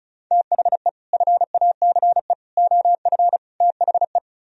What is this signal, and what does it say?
Morse code, heard as a keyed tone, about 44 wpm, 700 Hz. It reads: THE FACE OF THE